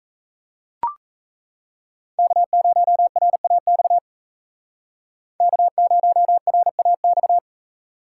Morse code, heard as a keyed tone, about 36 wpm, 700 Hz. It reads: K0RAX K0RAX